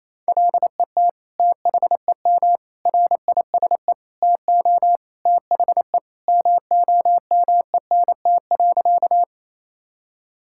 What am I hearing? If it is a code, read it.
Morse code, heard as a keyed tone, about 28 wpm, 700 Hz. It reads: LET THEM RISE TO THE MOMENT.
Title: Sentences from Top 1000 Words - Review Part 6 - 28wpm